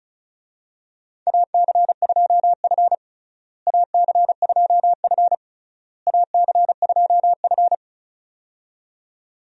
Morse code, heard as a keyed tone, about 36 wpm, 700 Hz. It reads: AC2F AC2F AC2F
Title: USA Call Signs - 4 Characters - Repeated 3x - 35wpm